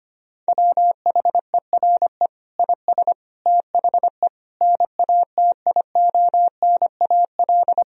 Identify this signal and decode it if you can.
Morse code, heard as a keyed tone, about 25 wpm, 700 Hz. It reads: WHERE IS THE NATIONAL
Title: Sentences from Top 1000 Words - Review Part 4 - 25wpm